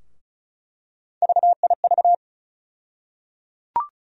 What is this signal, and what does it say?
Morse code, heard as a keyed tone, about 36 wpm, 700 Hz. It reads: VIV
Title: USA Call Sign Postfixes - 3 Letters 35wpm